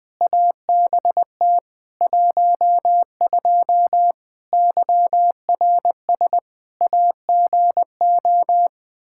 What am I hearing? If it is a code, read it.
Morse code, heard as a keyed tone, about 21 wpm, 700 Hz. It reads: ABT 12 YRS AGO
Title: K1USN SST QSO Practice 20wpm